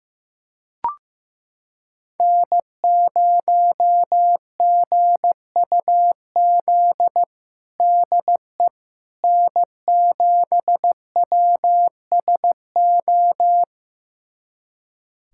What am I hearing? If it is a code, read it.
Morse code, heard as a keyed tone, about 15 wpm, 700 Hz. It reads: N0GUZ DE N7WSO
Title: USA Callsign DE Callsign - 5 Characters - 15wpm